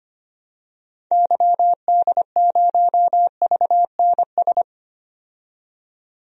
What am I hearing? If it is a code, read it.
Morse code, heard as a keyed tone, about 25 wpm, 700 Hz. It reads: YD0VNS
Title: DXCC-100 Call Signs - 6 Characters - 25wpm